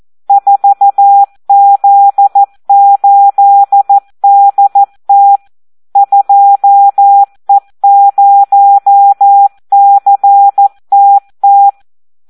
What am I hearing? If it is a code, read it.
Morse code, heard as a keyed tone, about 14 wpm, 800 Hz. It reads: 4Z8DT 2E0CTT
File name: Audio practica CW / SD 1 14